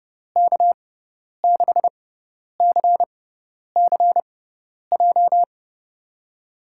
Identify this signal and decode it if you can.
Morse code, heard as a keyed tone, about 30 wpm, spaced as for 10 wpm, 700 Hz. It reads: K6CCJ
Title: USA Call Signs - 6 Characters - 6x Character Spacing - 30wpm